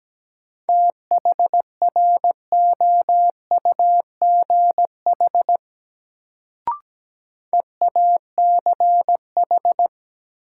Morse code, heard as a keyed tone, about 17 wpm, 700 Hz. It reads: THROUGH EACH
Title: Sets of 2 Words - New Word Drill - 17wpm